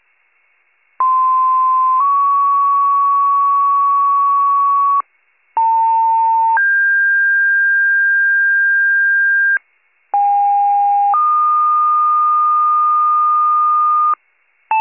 Two-tone page, heard as a one-second tone, then a three-second tone, about 1040 then 1120 Hz, about 900 then 1600 Hz, about 810 then 1220 Hz. It is heard from 1.0 to 14.1 s.